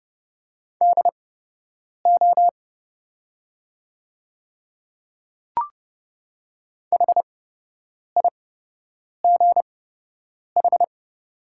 Morse code, heard as a keyed tone, about 31 wpm, spaced as for 8 wpm, 700 Hz. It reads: DO HIGH
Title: 8x Character Spacing - Top 200 Words - 30wpm